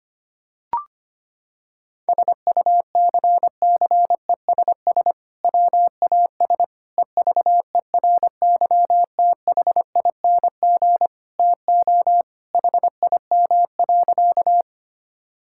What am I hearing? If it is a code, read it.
Morse code, heard as a keyed tone, about 25 wpm, 700 Hz. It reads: SUCCESS WAS EVERYTHING TO HIM.
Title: Sentences from Top 1000 Words - Review Part 6 - 25wpm